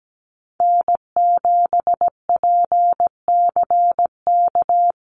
Morse code, heard as a keyed tone, about 17 wpm, 700 Hz. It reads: N7PCK
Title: USA Call Signs Repeated 2x 17wpm